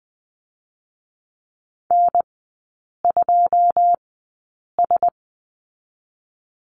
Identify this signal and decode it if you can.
Morse code, heard as a keyed tone, about 19 wpm, 700 Hz. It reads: N 2 S
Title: 114 Lesson 19e - Letter w - Three Characters 20wpm